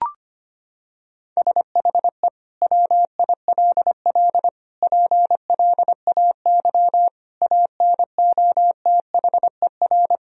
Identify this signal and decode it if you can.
Morse code, heard as a keyed tone, about 25 wpm, 700 Hz. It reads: SHE WILL PLAY ANOTHER
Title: Sentences from Top 1000 Words - Review Part 4 - 25wpm